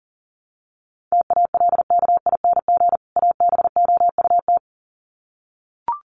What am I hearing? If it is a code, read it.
Morse code, heard as a keyed tone, about 40 wpm, 700 Hz. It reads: TALKING ABOUT
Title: Sets of 2 Words 40wpm